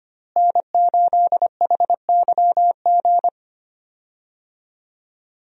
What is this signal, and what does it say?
Morse code, heard as a keyed tone, about 25 wpm, 700 Hz. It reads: N8HYG
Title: USA Call Signs - 5 Characters - Repeated 2x - 25wpm